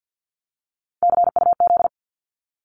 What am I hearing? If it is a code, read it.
Morse code, heard as a keyed tone, about 49 wpm, 700 Hz. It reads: CUZ
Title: Abbreviations Repeated 3x - 50wpm